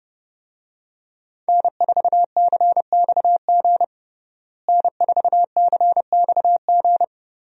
Morse code, heard as a keyed tone, about 30 wpm, 700 Hz. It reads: N4CXG N4CXG
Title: USA Call Signs - 5 Characters - Repeated 3x - 30wpm